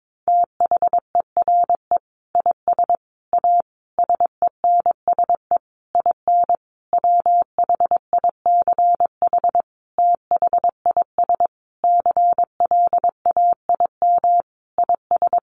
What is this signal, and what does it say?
Morse code, heard as a keyed tone, about 22 wpm, 700 Hz. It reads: THERE IS A SENSE IN WHICH THIS CLAIM IS
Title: Sentences from Top 1500 Words 22wpm